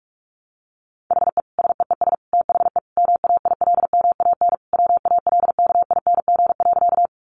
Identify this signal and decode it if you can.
Morse code, heard as a keyed tone, about 44 wpm, 700 Hz. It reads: HE SEES THE MAILMAN WALKING.
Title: Easy Phrases 45wpm